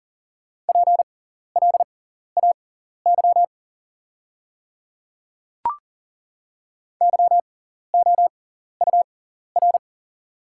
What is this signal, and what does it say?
Morse code, heard as a keyed tone, about 40 wpm, spaced as for 14 wpm, 700 Hz. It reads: PLAY YOUR
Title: 6x Character Spacing - Top 200 Words - 40wpm